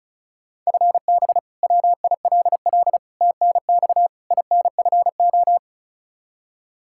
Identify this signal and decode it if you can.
Morse code, heard as a keyed tone, about 36 wpm, 700 Hz. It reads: FB WILL TNX INFO
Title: Practice QSO - Exchange 3 - More Info - 35wpm